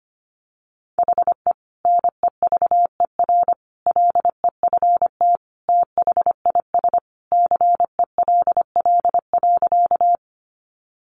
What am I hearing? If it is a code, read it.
Morse code, heard as a keyed tone, about 25 wpm, 700 Hz. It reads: HE NEVER LEFT THIS CELL.
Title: Sentences from Top 1100 Words 25wpm